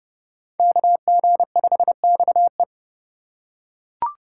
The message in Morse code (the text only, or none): KG5XE